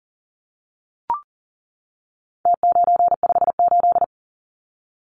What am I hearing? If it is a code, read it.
Morse code, heard as a keyed tone, about 39 wpm, 700 Hz. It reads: T958
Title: Numbers - Contest Style - 4-Digit 40wpm